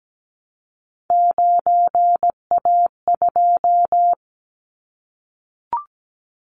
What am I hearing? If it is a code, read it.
Morse code, heard as a keyed tone, about 17 wpm, 700 Hz. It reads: 9A2
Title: Callsign Prefixes for Top 100 DXCC - 17wpm